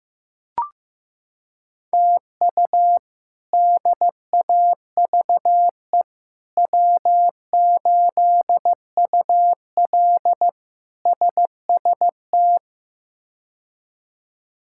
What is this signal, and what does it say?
Morse code, heard as a keyed tone, about 15 wpm, 700 Hz. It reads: TU DAVE W8UL SST
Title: K1USN SST QSO Practice - 15wpm